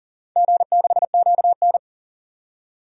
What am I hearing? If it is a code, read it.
Morse code, heard as a keyed tone, about 38 wpm, 700 Hz. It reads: G6QN
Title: DXCC-100 Call Signs - 4 Characters - 40wpm